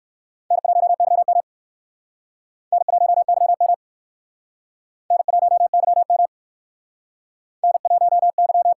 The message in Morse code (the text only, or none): N1YM N1YM N1YM N1Y